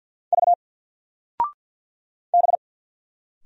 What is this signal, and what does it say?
Morse code, heard as a keyed tone, about 53 wpm, 700 Hz. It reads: V B
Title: Single Letter-Number 50wpm